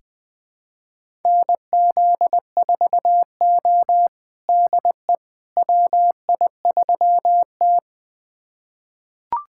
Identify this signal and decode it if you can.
Morse code, heard as a keyed tone, about 20 wpm, 700 Hz. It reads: NZ4O DE WI3T
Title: USA Callsign DE Callsign - 4 Characters - 20wpm